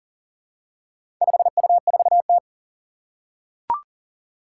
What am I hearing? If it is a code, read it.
Morse code, heard as a keyed tone, about 40 wpm, 700 Hz. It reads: 5U4T
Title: Four Challenging Letter-Number Combinations 40wpm